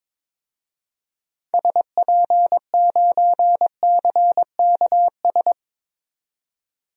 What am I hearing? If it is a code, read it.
Morse code, heard as a keyed tone, about 22 wpm, 700 Hz. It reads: SP9CKS
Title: DXCC-100 Call Signs - 6 Characters - 22wpm